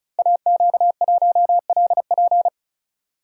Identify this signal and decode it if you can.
Morse code, heard as a keyed tone, about 36 wpm, 700 Hz. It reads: AQ1LP
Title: Code Groups - 5 Letter:Numbers - 35wpm